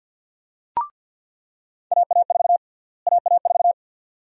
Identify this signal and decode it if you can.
Morse code, heard as a keyed tone, about 48 wpm, 700 Hz. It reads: AA4 AA4
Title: USA Call Sign Prefixes - Repeated 2x - 50wpm